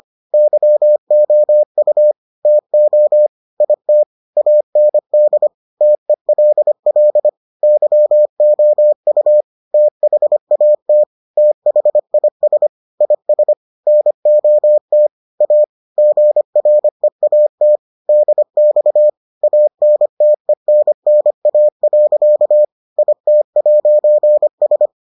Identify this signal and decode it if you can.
Morse code, heard as a keyed tone, about 25 wpm, 600 Hz. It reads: YOU TO IT AND TELL YOU THAT THIS IS NOT A GREAT DX ANTENNA. IT'S